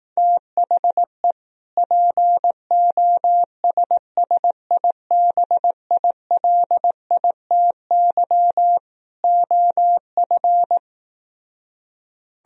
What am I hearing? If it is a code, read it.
Morse code, heard as a keyed tone, about 18 wpm, 700 Hz. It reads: THE POSSIBILITY OF
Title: Speed-Racing - Sets of 3 Words 15wpm